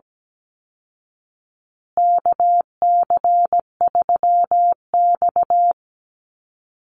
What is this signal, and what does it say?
Morse code, heard as a keyed tone, about 17 wpm, 700 Hz. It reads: KC3X
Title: Call Signs Repeated 2x - 17wpm